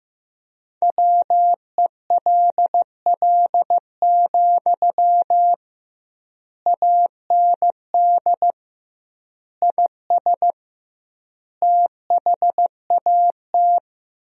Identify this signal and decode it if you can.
Morse code, heard as a keyed tone, about 15 wpm, 700 Hz. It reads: WELL, AND IS THAT
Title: 2x Word-Spacing - Sentences from Top 200 Words - 15wpm